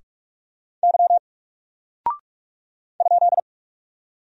Morse code, heard as a keyed tone, about 44 wpm, 700 Hz. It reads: Y ?
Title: Single Character (Letter:Number:Punctuation:Prosign) Alt 45wpm